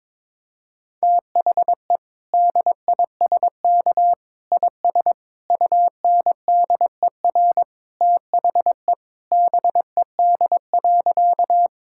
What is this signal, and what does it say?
Morse code, heard as a keyed tone, about 22 wpm, 700 Hz. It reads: THE DISK IS UNDER THE BED.
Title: Sentences from Top 1000 Words - Review Part 6 - 22wpm